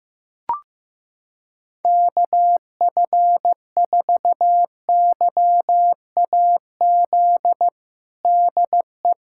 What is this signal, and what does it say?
Morse code, heard as a keyed tone, about 15 wpm, 700 Hz. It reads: KF4YAZ DE
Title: USA Callsign DE Callsign 15wpm